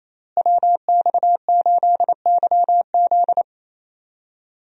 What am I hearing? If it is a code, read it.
Morse code, heard as a keyed tone, about 28 wpm, 700 Hz. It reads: WX8YZ
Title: USA Call Signs - 5 Characters - Repeated 2x - 28wpm